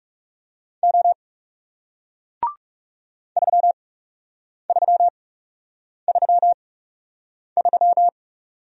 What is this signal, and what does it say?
Morse code, heard as a keyed tone, about 40 wpm, spaced as for 8 wpm, 700 Hz. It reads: O 3333